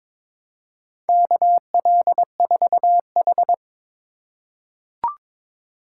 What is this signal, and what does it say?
Morse code, heard as a keyed tone, about 23 wpm, 700 Hz. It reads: KL4H
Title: USA Call Signs - 4 Characters - 4x Character Spacing - 22wpm